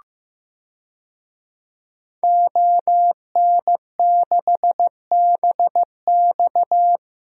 Morse code, heard as a keyed tone, about 15 wpm, 700 Hz. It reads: ON6BX
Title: DXCC-100 Call Signs - 5 Characters - 15wpm